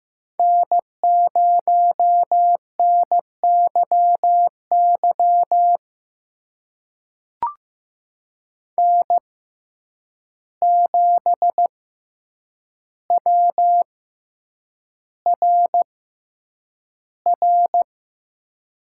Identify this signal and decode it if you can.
Morse code, heard as a keyed tone, about 15 wpm, spaced as for 8 wpm, 700 Hz. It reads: N0NYY N7WRR
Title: USA Call Signs - 5 Characters - 6x Character Spacing - 15wpm